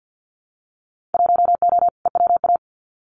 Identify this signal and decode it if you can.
Morse code, heard as a keyed tone, about 50 wpm, 700 Hz. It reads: 1O EWA
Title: Winter Field Day Exchanges Encore 50wpm